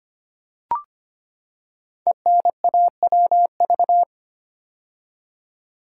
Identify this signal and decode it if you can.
Morse code, heard as a keyed tone, about 25 wpm, 700 Hz. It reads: ENAWV
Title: Code Groups - 5 Letters - 25wpm